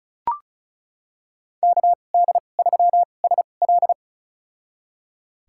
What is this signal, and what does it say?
Morse code, heard as a keyed tone, about 34 wpm, spaced as for 25 wpm, 700 Hz. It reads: KD3SL
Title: USA Call Signs - 5 Characters - 2x Character Spacing - 35wpm